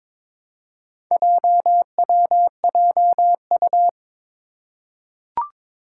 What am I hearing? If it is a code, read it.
Morse code, heard as a keyed tone, about 22 wpm, 700 Hz. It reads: JWJU
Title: Four Challenging Letter-Number Combinations 22wpm